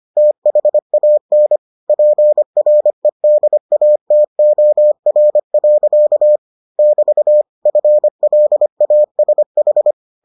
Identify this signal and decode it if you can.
Morse code, heard as a keyed tone, about 25 wpm, 600 Hz. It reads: THAN PREDATOR. <BT> FLASH